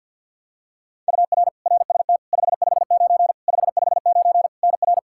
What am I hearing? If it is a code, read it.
Morse code, heard as a keyed tone, about 51 wpm, 700 Hz. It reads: UR RST 559 559 NR
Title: Practice QSO - Exchange 1 - Beginning - 50wpm